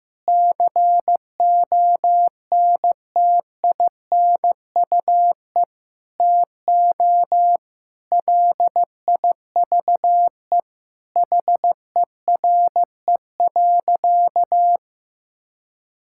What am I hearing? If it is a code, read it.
Morse code, heard as a keyed tone, about 15 wpm, 700 Hz. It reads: CONTINUE TO LIVE HERE.
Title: Sentences from Top 1000 Words - Review Part 6 - 15wpm